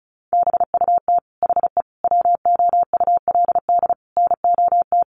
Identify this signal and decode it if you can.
Morse code, heard as a keyed tone, about 34 wpm, 700 Hz. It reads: BUT HE WOULD NOT
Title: Sentences from Top 1100 Words 35wpm